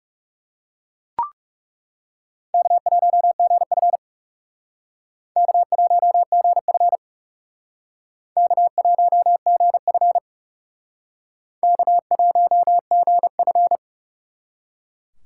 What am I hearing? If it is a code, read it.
Morse code, heard as a keyed tone, about 37 wpm, 700 Hz. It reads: K1GF K1GF K1GF K1GF
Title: Speed Racing - USA 1x2 Call Signs 30wpm